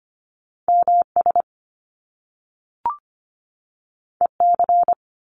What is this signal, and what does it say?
Morse code, heard as a keyed tone, about 25 wpm, 700 Hz. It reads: MS EC
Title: Top Two Letter Combinations in Words 25wpm